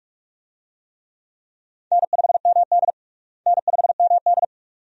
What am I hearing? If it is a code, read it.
Morse code, heard as a keyed tone, about 46 wpm, 700 Hz. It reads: N5MD N5MD
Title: USA Call Signs - 4 Characters - Repeated 2x - 45wpm